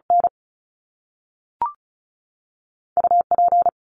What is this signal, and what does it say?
Morse code, heard as a keyed tone, about 36 wpm, 700 Hz. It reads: N U P
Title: Top Two Letter Combinations in Words 35wpm